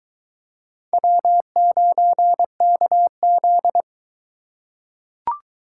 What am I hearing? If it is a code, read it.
Morse code, heard as a keyed tone, about 20 wpm, 700 Hz. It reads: W9KZ